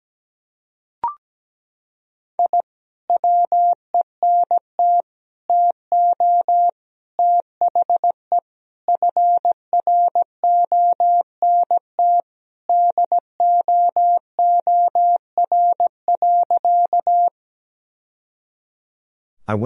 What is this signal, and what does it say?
Morse code, heard as a keyed tone, about 17 wpm, 700 Hz. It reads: I WENT TO THE FRONT DOOR.
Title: Sentences from Top 1000 Words - Review Part 2 - 17wpm